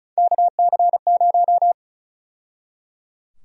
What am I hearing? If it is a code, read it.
Morse code, heard as a keyed tone, about 34 wpm, 700 Hz. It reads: KC0
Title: USA Call Sign Prefixes - 35wpm